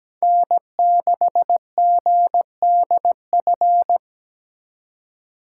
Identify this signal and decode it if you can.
Morse code, heard as a keyed tone, about 20 wpm, 700 Hz. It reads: N6GDF
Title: Speed Racing - USA 1x3 Call Signs 17wpm